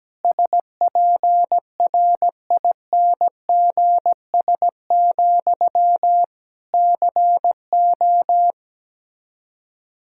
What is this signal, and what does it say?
Morse code, heard as a keyed tone, about 17 wpm, 700 Hz. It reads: SPRINGS, CO
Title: Practice QSO - Exchange 1 - Beginning - 17wpm